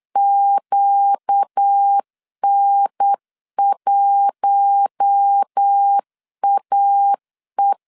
Morse code, heard as a keyed tone, about 8 wpm, 800 Hz. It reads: QN1AE
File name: CW / DET / 1 6 1500C S B 8wpm 5142